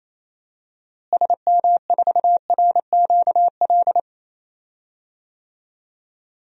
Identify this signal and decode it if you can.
Morse code, heard as a keyed tone, about 28 wpm, 700 Hz. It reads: SM4RQL
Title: DXCC-100 Call Signs - 6 Characters - Repeated 2x - 28wpm